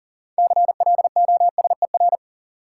Morse code, heard as a keyed tone, about 40 wpm, 700 Hz. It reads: CLOSER
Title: Sets of 5 Words Encore - New Word Drill - 40wpm